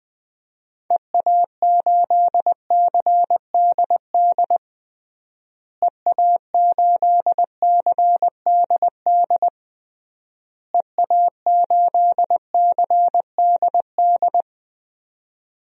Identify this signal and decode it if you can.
Morse code, heard as a keyed tone, about 20 wpm, 700 Hz. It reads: EA8CDD EA8CDD EA8CDD
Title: DXCC-100 Call Signs - 6 Characters - Repeated 3x - 20wpm